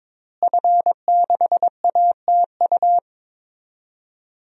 Morse code, heard as a keyed tone, about 22 wpm, 700 Hz. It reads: F6ATU
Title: DXCC-100 Call Signs - 5 Characters - Repeated 3x - 22wpm